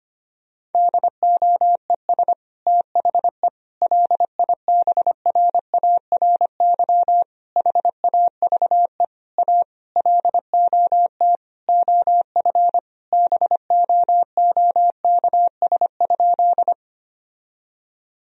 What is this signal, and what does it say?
Morse code, heard as a keyed tone, about 25 wpm, 700 Hz. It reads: DOES THE LIBRARY HAVE A LOT OF BOOKS?
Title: Sentences from Top 1000 Words - Review Part 4 - 25wpm